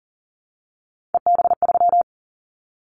Morse code, heard as a keyed tone, about 40 wpm, 700 Hz. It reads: EB3
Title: Callsign Prefixes for Top 25 DXCC - 40wpm